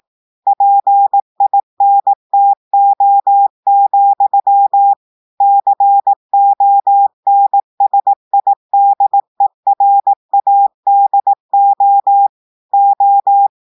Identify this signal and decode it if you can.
Morse code, heard as a keyed tone, about 18 wpm, 800 Hz. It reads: PINTO, CONSIDERADO O
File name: audios / portugais / portugais texte en clair 1 vitesse 18 mots minutes